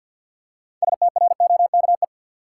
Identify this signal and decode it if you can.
Morse code, heard as a keyed tone, about 47 wpm, 700 Hz. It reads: STROKE
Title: Six Letter Words - 50wpm